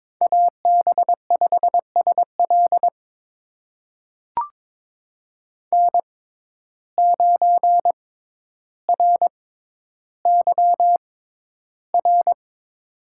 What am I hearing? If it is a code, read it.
Morse code, heard as a keyed tone, about 23 wpm, spaced as for 11 wpm, 700 Hz. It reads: AB5SL N9RYR
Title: USA Call Signs - 5 Characters - 6x Character Spacing - 22wpm